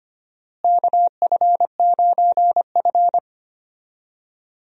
Morse code, heard as a keyed tone, about 25 wpm, 700 Hz. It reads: KF9F
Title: USA Call Signs - 4 Characters - Repeated 2x - 25wpm